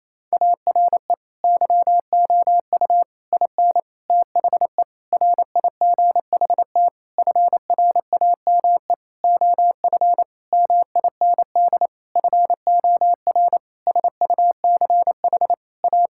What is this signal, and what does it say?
Morse code, heard as a keyed tone, about 28 wpm, 700 Hz. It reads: ARE YOU IN THE RIGHT FRAME OF MIND FOR SUCH A
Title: Sentences from Top 1000 Words - Review Part 6 - 28wpm